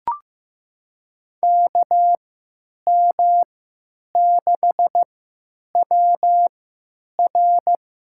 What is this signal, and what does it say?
Morse code, heard as a keyed tone, about 15 wpm, spaced as for 9 wpm, 700 Hz. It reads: KM6WR
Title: USA Call Signs - 3x Character Spacing - 15wpm